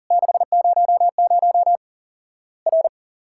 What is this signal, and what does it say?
Morse code, heard as a keyed tone, about 40 wpm, 700 Hz. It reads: 600 R
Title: Worked All Europe DX Contest - QTC Exchange - 40wpm